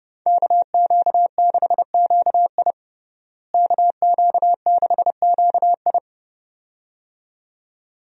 Morse code, heard as a keyed tone, about 30 wpm, 700 Hz. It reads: KQ6QI KQ6QI
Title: USA Call Signs - 5 Characters - Repeated 2x - 30wpm